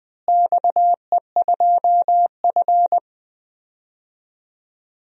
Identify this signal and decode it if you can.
Morse code, heard as a keyed tone, about 21 wpm, 700 Hz. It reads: XE2F
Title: DXCC-100 Call Signs - 4 Characters - Repeated 3x - 20wpm